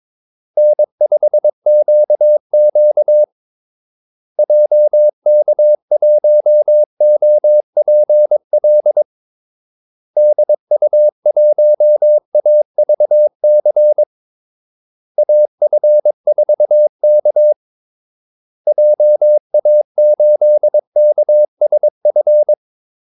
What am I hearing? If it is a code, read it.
Morse code, heard as a keyed tone, about 22 wpm, 600 Hz. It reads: N5QQ JK1OPL DU1AVC AF4K JA8KSF
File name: Callsigns-22wpm